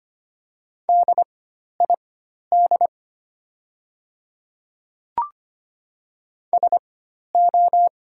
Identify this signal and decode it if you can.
Morse code, heard as a keyed tone, about 25 wpm, spaced as for 12 wpm, 700 Hz. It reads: DID SO
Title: 4x Character Spacing - Top 100 Words - 25wpm